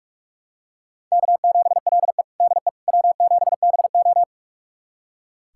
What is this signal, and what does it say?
Morse code, heard as a keyed tone, about 41 wpm, 700 Hz. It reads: K7LE DE W7BO
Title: USA Callsign DE Callsign - 4 Characters - 45wpm